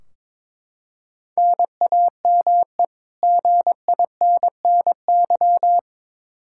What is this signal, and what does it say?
Morse code, heard as a keyed tone, about 22 wpm, 700 Hz. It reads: NAME GINNY
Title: Practice QSO - Exchange 1 - Beginning - 22wpm